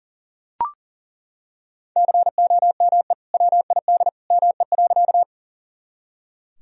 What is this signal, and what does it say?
Morse code, heard as a keyed tone, about 40 wpm, 700 Hz. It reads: COME WID ME.
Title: Sentences from Top 200 Words and Abbreviations 40wpm